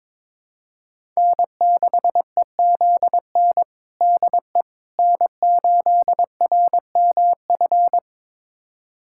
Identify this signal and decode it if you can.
Morse code, heard as a keyed tone, about 21 wpm, 700 Hz. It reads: N6EZN DE N8RMF
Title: USA Callsign DE Callsign - 5 Characters - 22wpm